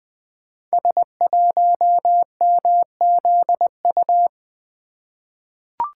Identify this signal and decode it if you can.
Morse code, heard as a keyed tone, about 20 wpm, 700 Hz. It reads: S1MZU
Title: Code Groups - 5 Letter:Numbers - 20wpm